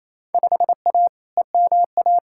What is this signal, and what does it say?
Morse code, heard as a keyed tone, about 28 wpm, 700 Hz. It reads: 5A EMA
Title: Field Day Exchanges Encore - 28wpm